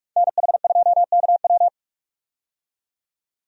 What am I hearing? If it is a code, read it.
Morse code, heard as a keyed tone, about 46 wpm, 700 Hz. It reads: NH2KW
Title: USA Call Signs - 5 Characters - 45wpm